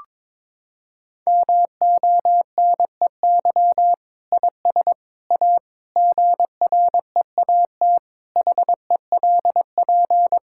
Morse code, heard as a keyed tone, about 22 wpm, 700 Hz. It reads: MONEY IS A GREAT HELP
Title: Sentences from Top 1600 Words 22wpm